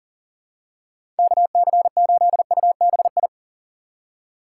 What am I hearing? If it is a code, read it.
Morse code, heard as a keyed tone, about 40 wpm, 700 Hz. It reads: KC8UBI
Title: USA Call Signs - 4x Character Spacing - 40wpm